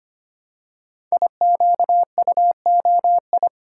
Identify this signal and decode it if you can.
Morse code, heard as a keyed tone, about 25 wpm, 700 Hz. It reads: IQUOI
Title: Code Groups - 5 Letters - 25wpm